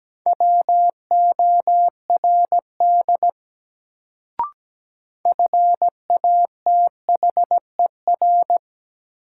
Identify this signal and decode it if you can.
Morse code, heard as a keyed tone, about 17 wpm, 700 Hz. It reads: WORD FATHER
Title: Sentences from Top 300 Words - New Word Drill - 17wpm